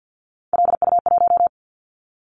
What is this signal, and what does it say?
Morse code, heard as a keyed tone, about 50 wpm, 700 Hz. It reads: LU1
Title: Callsign Prefixes for Top 100 DXCC - 50wpm